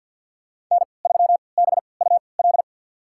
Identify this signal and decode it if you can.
Morse code, heard as a keyed tone, about 50 wpm, spaced as for 28 wpm, 700 Hz. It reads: N3BUL
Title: USA Call Signs - 5 Characters - 3x Character Spacing - 50wpm